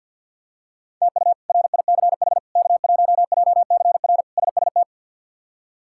Morse code, heard as a keyed tone, about 52 wpm, 700 Hz. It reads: TU RICH K1JCR SST
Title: K1USN SST QSO Practice - 50wpm